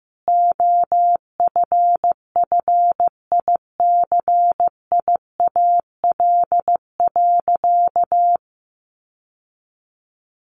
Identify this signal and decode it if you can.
Morse code, heard as a keyed tone, about 15 wpm, 700 Hz. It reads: OFFICIAL.
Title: Sentences from Top 1000 Words 15wpm